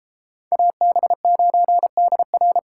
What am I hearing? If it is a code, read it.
Morse code, heard as a keyed tone, about 32 wpm, 700 Hz. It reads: AB9DR